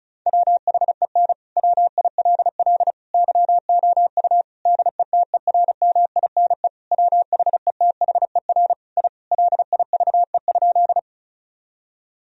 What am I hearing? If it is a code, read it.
Morse code, heard as a keyed tone, about 34 wpm, 700 Hz. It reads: WHEN WILL YOU DETERMINE WHETHER I LIVE?
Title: Sentences from Top 1000 Words - Review Part 4 - 35wpm